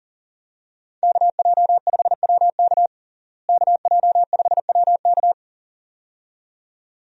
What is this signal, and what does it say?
Morse code, heard as a keyed tone, about 40 wpm, 700 Hz. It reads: KJ5WK KJ5WK
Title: USA Call Signs - 5 Characters - Repeated 2x - 40wpm